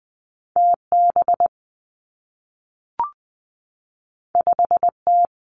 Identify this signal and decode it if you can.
Morse code, heard as a keyed tone, about 20 wpm, 700 Hz. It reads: T B 5 T
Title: Two Challenging Letter-Number Combinations 20wpm